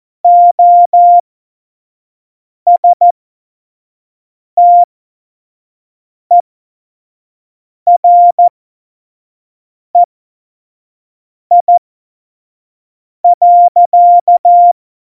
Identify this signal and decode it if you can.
Morse code, heard as a keyed tone, about 14 wpm, spaced as for 5 wpm, 700 Hz. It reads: OSTEREI.
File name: Morsekurs / Witze / Witz 28 April 2020, 25 F70